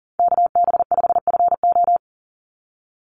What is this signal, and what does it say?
Morse code, heard as a keyed tone, about 40 wpm, 700 Hz. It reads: KB5FO
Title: USA Call Signs 40wpm